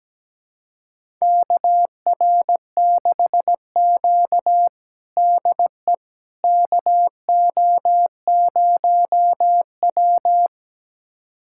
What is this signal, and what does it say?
Morse code, heard as a keyed tone, about 17 wpm, 700 Hz. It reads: KR6Q DE KO0W
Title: USA Callsign DE Callsign - 4 Characters - 17wpm